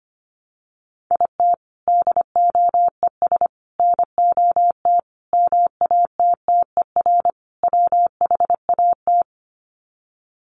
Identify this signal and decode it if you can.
Morse code, heard as a keyed tone, about 25 wpm, 690 Hz. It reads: IT DOES NOT MATTER WHAT
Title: Sets of 5 Words Encore 25wpm